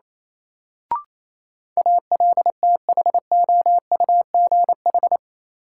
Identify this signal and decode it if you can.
Morse code, heard as a keyed tone, about 28 wpm, 700 Hz. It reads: ALTHOUGH
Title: Sentences from Top 600 Words - New Word Drill - 28wpm